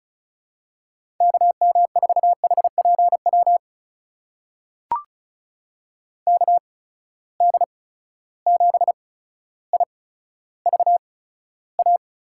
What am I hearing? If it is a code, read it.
Morse code, heard as a keyed tone, about 36 wpm, spaced as for 15 wpm, 700 Hz. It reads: KM4HPW KD7IVA
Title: USA Call Signs - 6 Characters - 8x Character Spacing - 35wpm